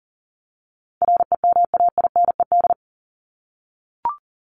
Morse code, heard as a keyed tone, about 40 wpm, 700 Hz. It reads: REMAINED
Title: ED Suffix - Top 100 Words with Base Words 40wpm